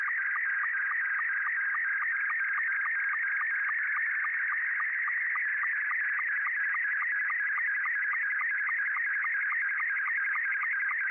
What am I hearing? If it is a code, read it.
SSTV (slow-scan television), Scottie 2.